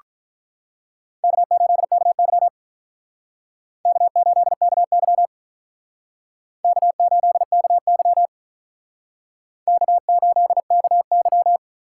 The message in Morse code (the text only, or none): K8KY K8KY K8KY K8KY